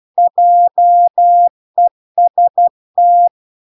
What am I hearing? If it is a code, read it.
Morse code, heard as a keyed tone, about 12 wpm, 700 Hz. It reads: JEST